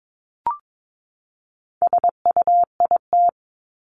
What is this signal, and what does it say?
Morse code, heard as a keyed tone, about 22 wpm, 700 Hz. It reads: SUIT
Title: Top 1501-1750 Words 22wpm